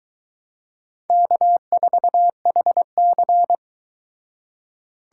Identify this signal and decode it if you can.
Morse code, heard as a keyed tone, about 23 wpm, 700 Hz. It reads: K4HC